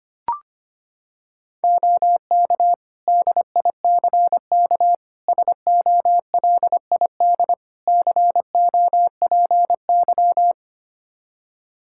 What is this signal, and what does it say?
Morse code, heard as a keyed tone, about 25 wpm, 700 Hz. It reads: OK DICK SOLID COPY
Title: Practice QSO - Exchange 3 - More Info - 25wpm